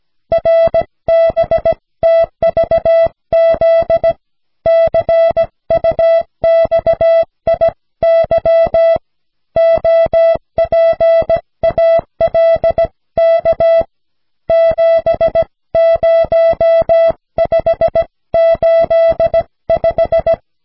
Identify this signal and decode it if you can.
Morse code, heard as a keyed tone, about 17 wpm, 660 Hz. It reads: RBTVZ CUXIY OPALK 70585
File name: cw / snelheid 4